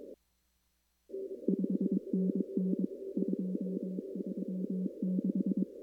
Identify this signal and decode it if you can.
Morse code, heard as a keyed tone, about 22 wpm, 190 Hz. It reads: 5NN 236
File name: CW / kt5j wpx cw-op-k5pi